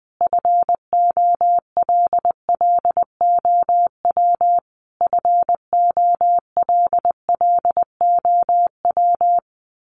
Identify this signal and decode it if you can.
Morse code, heard as a keyed tone, about 20 wpm, 700 Hz. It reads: FOLLOW FOLLOW
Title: Top 500 Words Repeated 20wpm